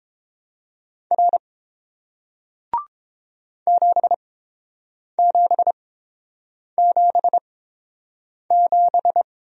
Morse code, heard as a keyed tone, about 32 wpm, spaced as for 7 wpm, 700 Hz. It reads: R 7777